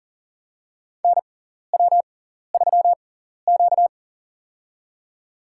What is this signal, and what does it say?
Morse code, heard as a keyed tone, about 40 wpm, spaced as for 14 wpm, 700 Hz. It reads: NW3Q
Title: USA Call Signs - 4 Characters - 6x Character Spacing - 40wpm